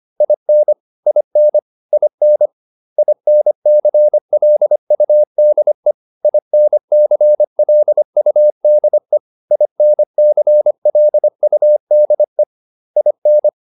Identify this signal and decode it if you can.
Morse code, heard as a keyed tone, about 25 wpm, 600 Hz. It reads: IN IN IN INCLUDE INCLUDE INCLUDE IN